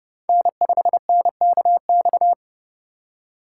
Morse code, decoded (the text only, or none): N5NKX